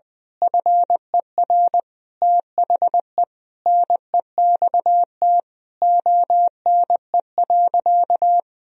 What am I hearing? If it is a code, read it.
Morse code, heard as a keyed tone, about 20 wpm, 700 Hz. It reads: FER THE NEXT ONE.
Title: Sentences from Top 400 Words and Abbreviations 20wpm